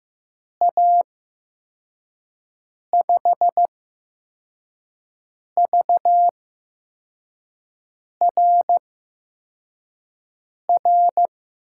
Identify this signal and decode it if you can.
Morse code, heard as a keyed tone, about 15 wpm, spaced as for 4 wpm, 700 Hz. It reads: A5VRR
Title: USA Call Signs - 6 Characters - 8x Character Spacing - 15wpm